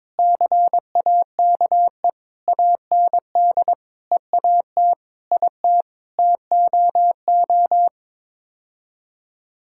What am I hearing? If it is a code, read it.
Morse code, heard as a keyed tone, about 21 wpm, 700 Hz. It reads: CAKE AND EAT IT TOO
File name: practice / 22wpm / Sets of 8 Words - 22wpm